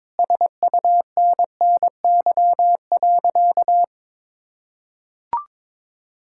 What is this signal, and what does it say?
Morse code, heard as a keyed tone, about 22 wpm, 700 Hz. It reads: SUNNY.
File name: practice / 22wpm / Sentences with 3-Word Phrases - 22wpm